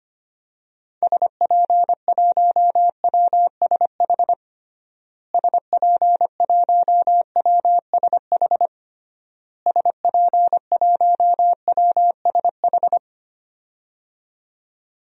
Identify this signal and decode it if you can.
Morse code, heard as a keyed tone, about 25 wpm, 700 Hz. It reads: SP1WSH SP1WSH SP1WSH
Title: DXCC-100 Call Signs - 6 Characters - Repeated 3x - 25wpm